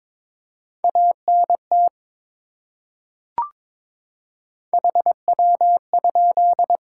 Morse code, heard as a keyed tone, about 22 wpm, 700 Hz. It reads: ANT HW?
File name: practice / 22wpm / Top 20 QSO Elements - 22wpm